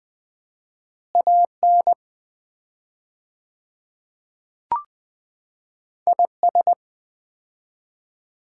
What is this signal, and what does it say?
Morse code, heard as a keyed tone, about 20 wpm, 700 Hz. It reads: AN IS
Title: 4x Word-Spacing - Sets of 2 Words - 20wpm